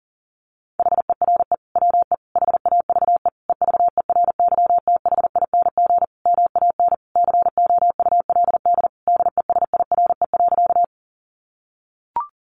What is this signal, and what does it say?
Morse code, heard as a keyed tone, about 40 wpm, 700 Hz. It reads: HERE WE HAVE EVERYTHING MAN COULD DESIRE.
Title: Sentences from Top 1100 Words 40wpm